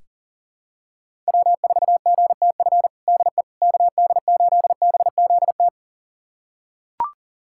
Morse code, heard as a keyed tone, about 40 wpm, 700 Hz. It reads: W4GTF DE KD8BZT
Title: USA Callsign DE Callsign 40wpm